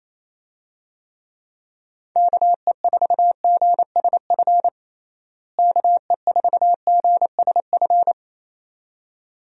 Morse code, heard as a keyed tone, about 28 wpm, 700 Hz. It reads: KE4GSF KE4GSF
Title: USA Call Signs - 6 Characters - Repeated 2x - 28wpm